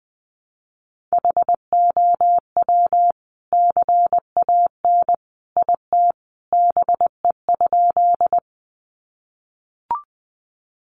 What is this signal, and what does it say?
Morse code, heard as a keyed tone, about 20 wpm, 700 Hz. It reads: HOW CAN IT BE?
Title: Sentences from Top 100 Words 20wpm